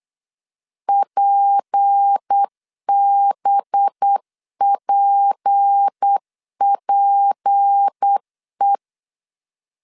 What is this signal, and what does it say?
Morse code, heard as a keyed tone, about 8 wpm, 800 Hz. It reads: PBPPE
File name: LEZIONI / 6 PKX0 / 6 P 8wpm 3025